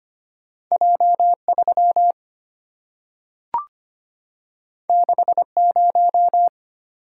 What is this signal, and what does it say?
Morse code, heard as a keyed tone, about 25 wpm, 700 Hz. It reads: J3 60
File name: Two Letter-Number Combinations - 25wpm